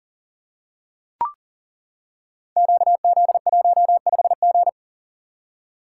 Morse code, heard as a keyed tone, about 39 wpm, 700 Hz. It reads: QZ15G
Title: Code Groups - 5 Letter:Numbers - 40wpm